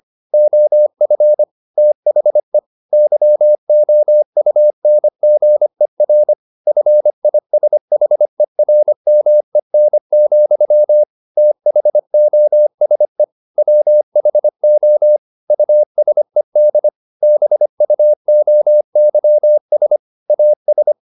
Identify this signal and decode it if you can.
Morse code, heard as a keyed tone, about 25 wpm, 600 Hz. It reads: OF THE YOUNGER FISHERMEN, THOSE WHO USED BUOYS AS
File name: the old man and the sea part 1 25wpm